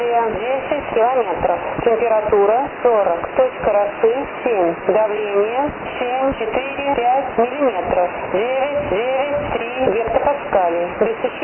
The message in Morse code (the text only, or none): none